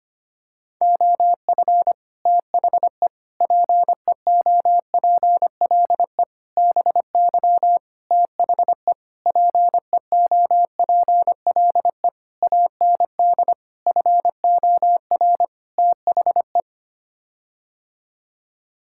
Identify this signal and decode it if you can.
Morse code, heard as a keyed tone, about 25 wpm, 700 Hz. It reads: OF THE PEOPLE BY THE PEOPLE AND FOR THE
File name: Sets of 9 Words - 25wpm